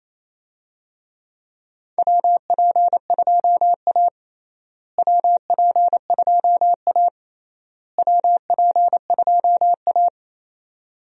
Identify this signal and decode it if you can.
Morse code, heard as a keyed tone, about 27 wpm, 700 Hz. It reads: WP2A WP2A WP2A
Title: USA Call Signs - 4 Characters - Repeated 3x - 28wpm